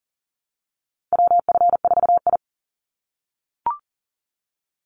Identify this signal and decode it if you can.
Morse code, heard as a keyed tone, about 40 wpm, 700 Hz. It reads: WF4I
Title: NA SOTA Chaser Callsigns 40wpm